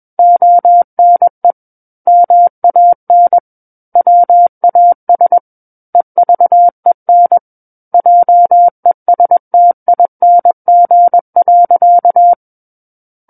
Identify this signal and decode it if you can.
Morse code, heard as a keyed tone, about 21 wpm, 700 Hz. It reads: ONE MAN WAS EVEN JESTING.
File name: Book One / 12-Chapter 12 - 21 WPM